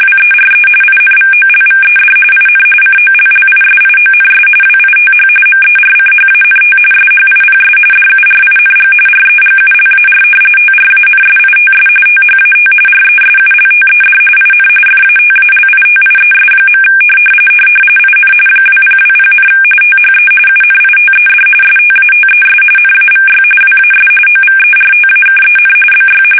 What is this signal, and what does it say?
RTTY, about 45.45 baud. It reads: WERE INVOLVED IN COLLISIONS WITH MERCHANT SHIPS DURING REGULAR TRANSITS THAT RESULTED IN FATALITIES. IN 2020, THE UNITED STATES NAVY WAS OVERTAKEN BY
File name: fsk850 / usnavy5